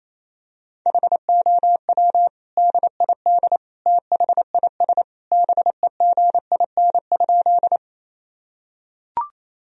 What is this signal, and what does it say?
Morse code, heard as a keyed tone, about 28 wpm, 700 Hz. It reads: HOW DID THIS BEGIN?
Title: Sentences from Top 1000 Words - Review Part 1 - 28wpm